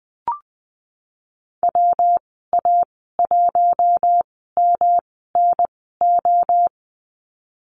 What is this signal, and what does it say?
Morse code, heard as a keyed tone, about 20 wpm, spaced as for 15 wpm, 700 Hz. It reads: WA1MNO
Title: 122 Lesson 20g - Number 3 - Call Signs 20wpm